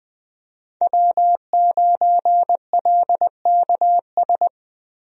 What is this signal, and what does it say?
Morse code, heard as a keyed tone, about 20 wpm, 700 Hz. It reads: W9LKS
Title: USA Call Signs - 5 Characters - Repeated 2x - 20wpm